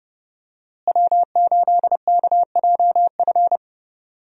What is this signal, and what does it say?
Morse code, heard as a keyed tone, about 32 wpm, 700 Hz. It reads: W8KJF
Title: USA Call Signs - 5 Characters - Repeated 3x - 30wpm